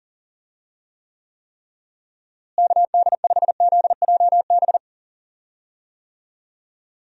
Morse code, heard as a keyed tone, about 38 wpm, 700 Hz. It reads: KD5ZJB
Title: USA Call Signs - 6 Characters - Repeated 3x - 40wpm